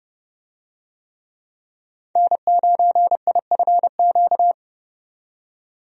Morse code, heard as a keyed tone, about 30 wpm, 700 Hz. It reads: N9IFQ
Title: USA Call Signs - 5 Characters - 30wpm